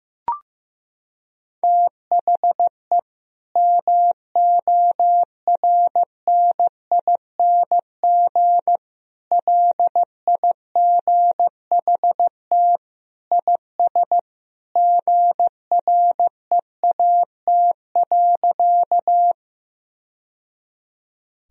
Morse code, heard as a keyed tone, about 15 wpm, 700 Hz. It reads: THE MORNING LIGHT IS GREAT.
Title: Sentences from Top 1000 Words - Review Part 1 - 15wpm